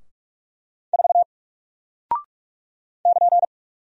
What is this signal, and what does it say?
Morse code, heard as a keyed tone, about 44 wpm, 700 Hz. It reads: 4 <KN>
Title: Single Character (Letter:Number:Punctuation:Prosign) Alt 45wpm